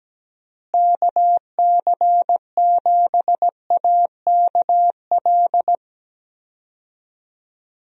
Morse code, heard as a keyed tone, about 17 wpm, 700 Hz. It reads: KC7AKL
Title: USA Call Signs - 6 Characters - Repeated 2x - 17wpm